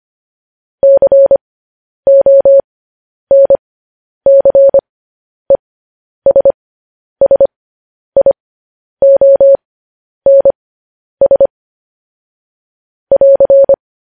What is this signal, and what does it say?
Morse code, heard as a keyed tone, about 25 wpm, spaced as for 10 wpm, 550 Hz. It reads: CONCESSIONS <AR>
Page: news headlines in morse code at 10 wpm